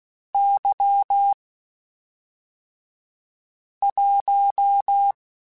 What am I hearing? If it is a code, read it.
Morse code, heard as a keyed tone, about 16 wpm, 800 Hz. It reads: Y 1